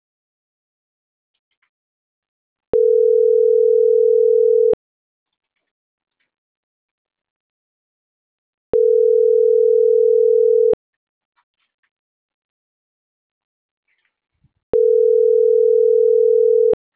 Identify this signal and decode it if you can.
Ringback tone (North America), 440 + 480 Hz.